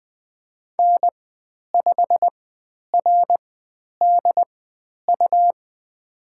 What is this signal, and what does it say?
Morse code, heard as a keyed tone, about 20 wpm, spaced as for 10 wpm, 700 Hz. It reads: N5RDU
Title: 109 Lesson 18g - Letter m - Call Signs 20wpm